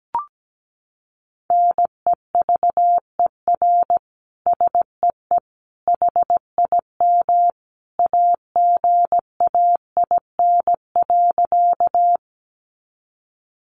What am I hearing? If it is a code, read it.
Morse code, heard as a keyed tone, about 17 wpm, 700 Hz. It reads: NEVER SEE HIM AGAIN.
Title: Sentences from Top 300 Words 17wpm